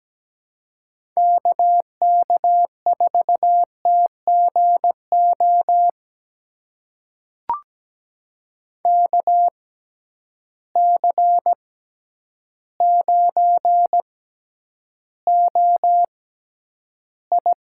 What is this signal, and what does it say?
Morse code, heard as a keyed tone, about 17 wpm, spaced as for 9 wpm, 700 Hz. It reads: KK4TGO KC9OI
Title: USA Call Signs - 6 Characters - 6x Character Spacing - 17wpm